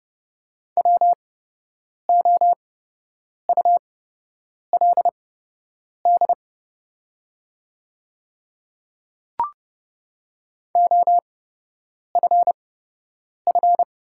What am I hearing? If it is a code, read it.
Morse code, heard as a keyed tone, about 31 wpm, spaced as for 8 wpm, 690 Hz. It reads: WOULD OFF
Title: 8x Character Spacing - Top 200 Words - 30wpm